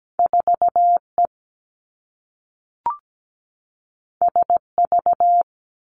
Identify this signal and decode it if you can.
Morse code, heard as a keyed tone, about 17 wpm, 700 Hz. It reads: 4E SV
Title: Two Challenging Letter-Number Combinations 17wpm